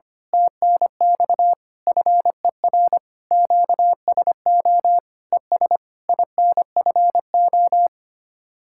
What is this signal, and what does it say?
Morse code, heard as a keyed tone, about 24 wpm, 700 Hz. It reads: TNX FER QSO ES INFO
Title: Speed-Racing Common QSO Phrases Encore - 25wpm